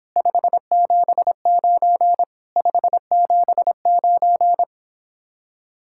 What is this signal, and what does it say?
Morse code, heard as a keyed tone, about 29 wpm, 700 Hz. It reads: 579 579